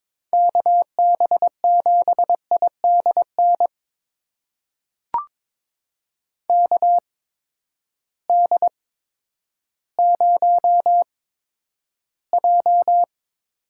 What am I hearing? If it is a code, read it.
Morse code, heard as a keyed tone, about 22 wpm, spaced as for 11 wpm, 700 Hz. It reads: KB7IDN KD0J
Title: USA Call Signs - 6 Characters - 8x Character Spacing - 22wpm